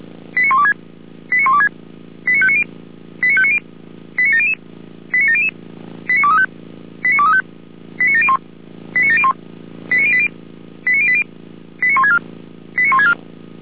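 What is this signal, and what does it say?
Five-tone selective call (ZVEI), heard as quick runs of five tones, tones of about 70 ms. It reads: 89126 89126 89590 89590 89700 89700 89235 89235 89701 89701 89080 89080 89164 89164
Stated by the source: ZVEI five-tone call 89120 received by SDR.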